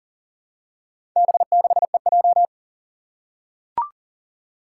Morse code, heard as a keyed tone, about 37 wpm, 700 Hz. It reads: B6EJ
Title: Four Challenging Letter-Number Combinations 40wpm